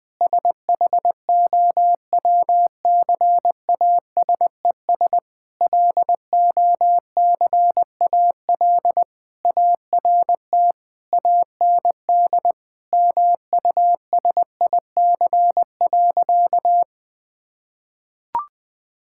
Morse code, heard as a keyed tone, about 21 wpm, 700 Hz. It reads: SHOWCASES LOCAL ART AND MUSIC.
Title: Difficult Sentences with Hints - 20wpm